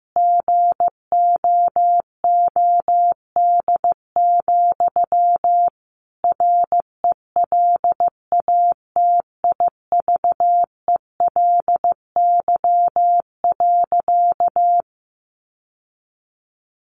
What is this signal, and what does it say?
Morse code, heard as a keyed tone, about 15 wpm, 700 Hz. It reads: GOOD, RELATIVELY.
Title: Sentences from Top 1100 Words 15wpm